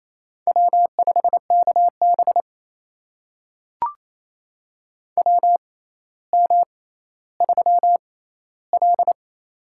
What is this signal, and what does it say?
Morse code, heard as a keyed tone, about 27 wpm, spaced as for 14 wpm, 700 Hz. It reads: W5KB WM3L
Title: USA Call Signs - 4 Characters - 6x Character Spacing - 28wpm